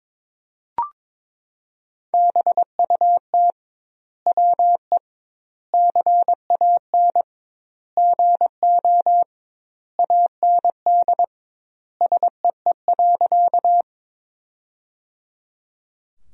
Morse code, heard as a keyed tone, about 22 wpm, 700 Hz. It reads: BUT WE CAN GO AND SEE.
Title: 2x Word-Spacing - Sentences from Top 100 Words - 22wpm